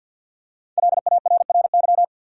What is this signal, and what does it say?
Morse code, heard as a keyed tone, about 46 wpm, 700 Hz. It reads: LARRY
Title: USA Common Names - 50wpm